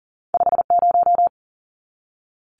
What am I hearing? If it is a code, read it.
Morse code, heard as a keyed tone, about 40 wpm, 700 Hz. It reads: 50
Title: Numbers - 2-Digit 40wpm